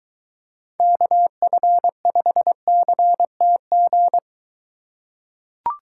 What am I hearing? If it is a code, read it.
Morse code, heard as a keyed tone, about 22 wpm, 700 Hz. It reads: KF5CTG